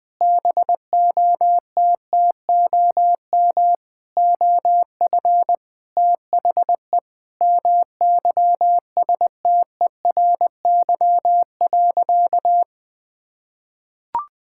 Morse code, heard as a keyed tone, about 21 wpm, 700 Hz. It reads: BOTTOM OF THE MYSTERY.
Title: Sentences with 6-Word Phrases - Spoken Beginning - 20wpm